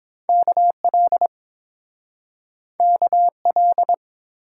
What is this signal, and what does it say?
Morse code, heard as a keyed tone, about 31 wpm, 700 Hz. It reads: KL KL